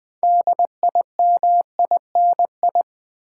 Morse code, heard as a keyed tone, about 20 wpm, 700 Hz. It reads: DIMINI
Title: Build A Word - 3 Letters At A Time - 8-Letter Words - 20wpm